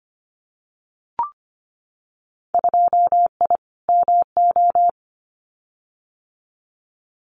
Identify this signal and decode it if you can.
Morse code, heard as a keyed tone, about 25 wpm, 700 Hz. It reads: 2I MO
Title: Winter Field Day Exchanges 25wpm